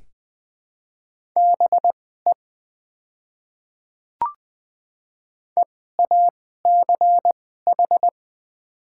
Morse code, heard as a keyed tone, about 20 wpm, spaced as for 14 wpm, 700 Hz. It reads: BE EACH
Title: 2x Character Spacing - Top 500 Words - 20wpm